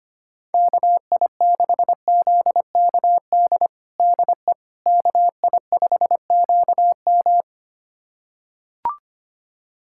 Morse code, heard as a keyed tone, about 25 wpm, 700 Hz. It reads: KI6ZKD DE KI5QM
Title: USA Callsign DE Callsign 25wpm